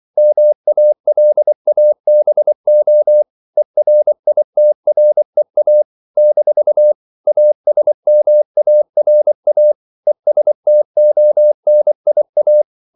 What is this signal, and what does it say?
Morse code, heard as a keyed tone, about 24 wpm, 600 Hz. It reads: MALABO ERITREA - ASMARA ESTONIA
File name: CW / Practice / NationalCapitals24